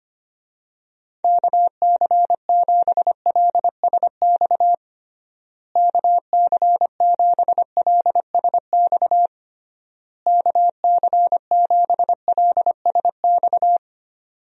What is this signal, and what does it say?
Morse code, heard as a keyed tone, about 25 wpm, 700 Hz. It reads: KC7LSX KC7LSX KC7LSX
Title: USA Call Signs - 6 Characters - Repeated 3x - 25wpm